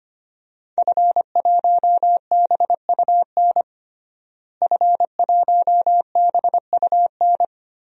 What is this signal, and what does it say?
Morse code, heard as a keyed tone, about 25 wpm, 700 Hz. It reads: F1BUN F1BUN
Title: DXCC-100 Call Signs - 5 Characters - Repeated 3x - 25wpm